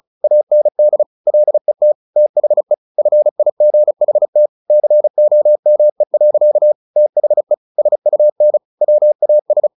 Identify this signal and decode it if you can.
Morse code, heard as a keyed tone, about 34 wpm, 600 Hz. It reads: AND LET THE FIGHT COME. THE SUN WAS
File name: the old man and the sea 2 35wpm